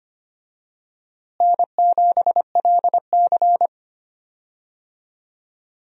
Morse code, heard as a keyed tone, about 25 wpm, 700 Hz. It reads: N7LC
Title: USA Call Signs - 4 Characters - Repeated 2x - 25wpm